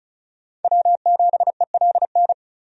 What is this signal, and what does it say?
Morse code, heard as a keyed tone, about 34 wpm, 700 Hz. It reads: W7ELN
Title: USA Call Signs - 5 Characters - 35wpm